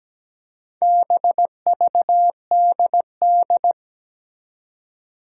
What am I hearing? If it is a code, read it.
Morse code, heard as a keyed tone, about 17 wpm, 700 Hz. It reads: BVDD
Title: Four Challenging Letter-Number Combinations 17wpm